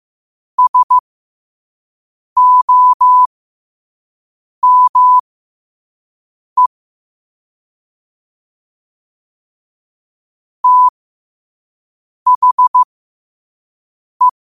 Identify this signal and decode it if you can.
Morse code, heard as a keyed tone, about 15 wpm, spaced as for 5 wpm, 1000 Hz. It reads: SOME THE
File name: qotdfiles / 5wpm